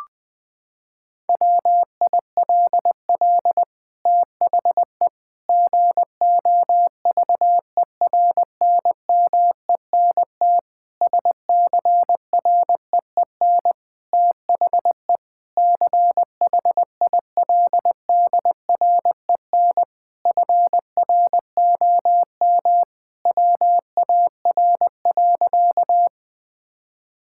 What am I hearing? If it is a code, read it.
Morse code, heard as a keyed tone, about 20 wpm, 700 Hz. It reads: WILL THE GOVERNMENT SCREEN THE CHILDREN FROM WAR.
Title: Sentences from Top 1000 Words - Review Part 6 - 20wpm